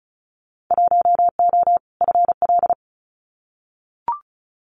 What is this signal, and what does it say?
Morse code, heard as a keyed tone, about 34 wpm, 700 Hz. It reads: 1O FL
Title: Winter Field Day Exchanges 35wpm